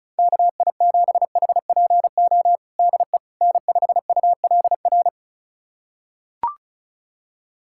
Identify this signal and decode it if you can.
Morse code, heard as a keyed tone, about 34 wpm, 700 Hz. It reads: KI7HPO DE N5ULR
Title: USA Callsign DE Callsign 35wpm